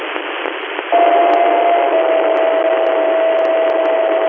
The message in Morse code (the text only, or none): none